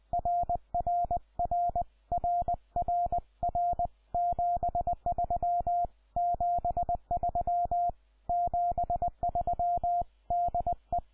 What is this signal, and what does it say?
Morse code, heard as a keyed tone, about 20 wpm, 700 Hz. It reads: RRR RR R 73 73 73 DE
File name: iw0ffk / 2014 03 19 ZS1NAZ IW0FFK